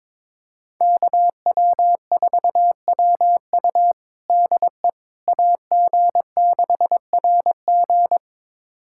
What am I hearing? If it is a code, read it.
Morse code, heard as a keyed tone, about 21 wpm, 700 Hz. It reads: KW4WU DE AG6RG
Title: USA Callsign DE Callsign - 5 Characters - 22wpm